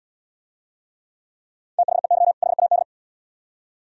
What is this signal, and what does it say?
Morse code, heard as a keyed tone, about 72 wpm, 700 Hz. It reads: THEY HAD